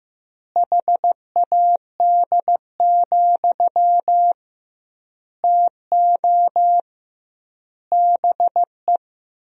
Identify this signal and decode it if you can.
Morse code, heard as a keyed tone, about 15 wpm, 700 Hz. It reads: HAD, TO BE
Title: 2x Word-Spacing - Sentences from Top 500 Words - 15wpm